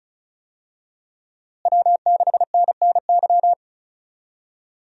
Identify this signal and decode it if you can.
Morse code, heard as a keyed tone, about 34 wpm, 700 Hz. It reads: W6NNY
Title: USA Call Signs - 5 Characters - Repeated 3x - 35wpm